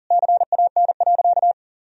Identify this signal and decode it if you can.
Morse code, heard as a keyed tone, about 37 wpm, 700 Hz. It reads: CAN.
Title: Sentences from Top 1000 Words - Review Part 2 - 40wpm